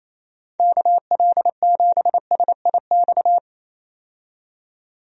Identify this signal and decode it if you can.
Morse code, heard as a keyed tone, about 28 wpm, 700 Hz. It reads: KL7SIX
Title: DXCC-100 Call Signs - 6 Characters - 28wpm